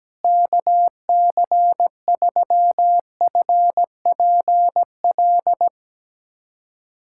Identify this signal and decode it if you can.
Morse code, heard as a keyed tone, about 17 wpm, 700 Hz. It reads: KC3FPL
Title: USA Callsign DE Callsign - 6 Characters - 17wpm